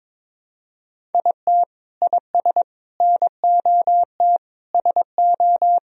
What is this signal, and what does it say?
Morse code, heard as a keyed tone, about 22 wpm, 700 Hz. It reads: IT IS NOT SO
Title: Sentences from Top 1000 Words - Review Part 1 - 22wpm